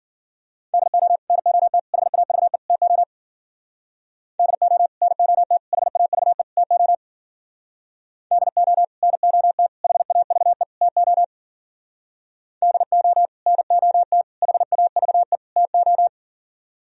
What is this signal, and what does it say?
Morse code, heard as a keyed tone, about 55 wpm, 700 Hz. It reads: DO NOT HAVE TO DO NOT HAVE TO DO NOT HAVE TO DO NOT HAVE TO